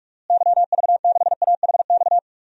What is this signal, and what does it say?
Morse code, heard as a keyed tone, about 46 wpm, 700 Hz. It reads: YV6AHX
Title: DXCC-100 Call Signs - 6 Characters - Repeated 3x - 45wpm